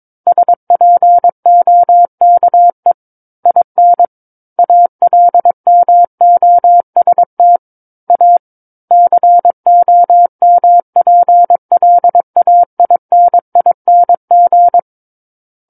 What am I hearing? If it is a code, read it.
Morse code, heard as a keyed tone, about 22 wpm, 700 Hz. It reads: SPOKE IN ALMOST A COMPLAINING
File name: Book One / 13-Chapter 13 - 22 WPM